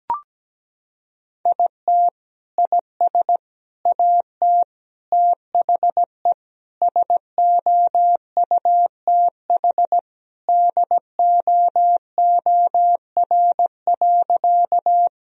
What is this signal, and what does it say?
Morse code, heard as a keyed tone, about 17 wpm, 700 Hz. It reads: IT IS AT THE SOUTH DOOR.
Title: Sentences from Top 1000 Words - Review Part 2 - 17wpm